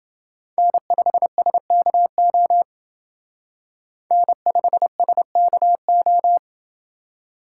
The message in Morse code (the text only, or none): N5SKO N5SKO